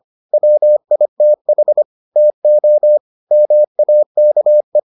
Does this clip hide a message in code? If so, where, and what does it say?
Yes: Morse code from 0.3 to 4.8 s, heard as a keyed tone, about 25 wpm, 600 Hz. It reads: WITH TO MAKE